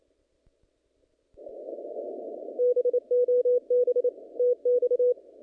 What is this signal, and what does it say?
Morse code, heard as a keyed tone, about 27 wpm, 490 Hz. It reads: BOB TX
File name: contests / naqp-cw-k5tr-op-k5pi- -0000